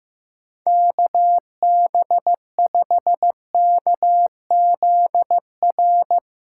Morse code, heard as a keyed tone, about 15 wpm, 700 Hz. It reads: KB5KZR
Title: USA Callsign DE Callsign - 6 Characters - 15wpm